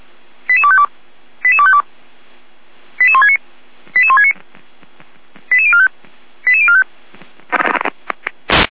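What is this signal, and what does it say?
Five-tone selective call (ZVEI), heard as quick runs of five tones, tones of about 70 ms. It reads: 88252 88252 88169 88169 88045 88045
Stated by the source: Received on an SDR, ZVEI five-tone code 88255.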